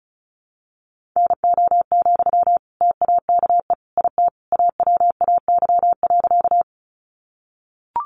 Morse code, heard as a keyed tone, about 34 wpm, 700 Hz. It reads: NO, TAKE IT AWAY.
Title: Sentences from Top 600 Words 35wpm